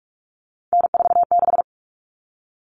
Morse code, heard as a keyed tone, about 44 wpm, 700 Hz. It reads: N46
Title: All Cut Numbers - 3-Digit 45wpm